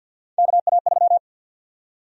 Morse code, heard as a keyed tone, about 47 wpm, 700 Hz. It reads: KA3
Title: USA Call Sign Prefixes - 50wpm